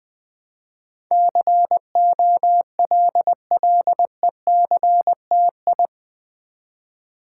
Morse code, heard as a keyed tone, about 20 wpm, 700 Hz. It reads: COLLECTI
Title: Build A Word - 4 Letters At A Time - 10-Letter Words - 20wpm